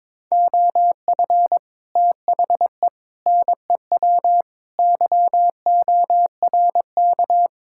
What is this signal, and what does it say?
Morse code, heard as a keyed tone, about 23 wpm, 700 Hz. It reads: OF THE NEW YORK
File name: practice / 22wpm / Build A Phrase - 5-Word Phrases - 22wpm